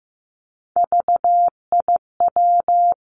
Morse code, heard as a keyed tone, about 15 wpm, 700 Hz. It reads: VIW
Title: Three Challenging Letter-Number Combinations 15wpm